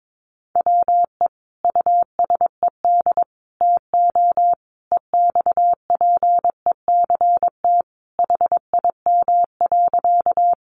Morse code, heard as a keyed tone, about 22 wpm, 700 Hz. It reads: WE USED TO EXPECT HIM.
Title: Sentences from Top 1100 Words 22wpm